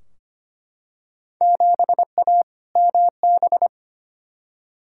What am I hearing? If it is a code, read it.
Morse code, heard as a keyed tone, about 25 wpm, 700 Hz. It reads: 7A MB
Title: Field Day Exchanges Encore - 25wpm